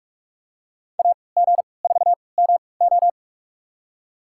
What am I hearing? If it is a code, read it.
Morse code, heard as a keyed tone, about 46 wpm, spaced as for 25 wpm, 700 Hz. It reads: AG4MO
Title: USA Call Signs - 5 Characters - 3x Character Spacing - 45wpm